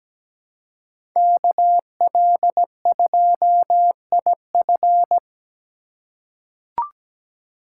Morse code, heard as a keyed tone, about 17 wpm, 700 Hz. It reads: KL2IF
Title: USA Call Signs - 4x Character Spacing - 17wpm